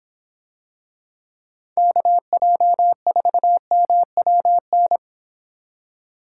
Morse code, heard as a keyed tone, about 25 wpm, 700 Hz. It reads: KJ4MWN